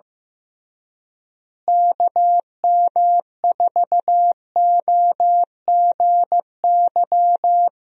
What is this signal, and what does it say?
Morse code, heard as a keyed tone, about 15 wpm, 700 Hz. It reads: KM4OGY
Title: USA Call Signs - 6 Characters - Repeated 3x - 15wpm